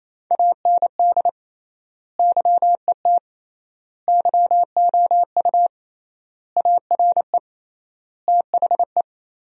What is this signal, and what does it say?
Morse code, heard as a keyed tone, about 28 wpm, 700 Hz. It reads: AND YET YOU ARE THE
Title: 3x Word-Spacing - Sentences from Top 500 Words - 28wpm